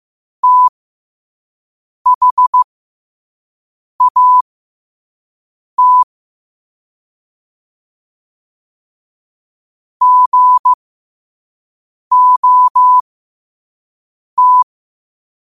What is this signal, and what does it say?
Morse code, heard as a keyed tone, about 15 wpm, spaced as for 5 wpm, 1000 Hz. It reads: THAT GOT
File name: qotdfiles / 5wpm